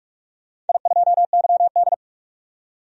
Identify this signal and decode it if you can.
Morse code, heard as a keyed tone, about 42 wpm, 700 Hz. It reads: I2YD
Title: DXCC-100 Call Signs - 4 Characters - 45wpm